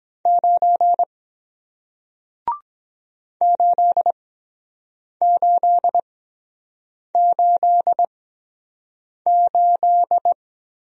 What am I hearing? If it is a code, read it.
Morse code, heard as a keyed tone, about 23 wpm, spaced as for 7 wpm, 700 Hz. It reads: 9 8888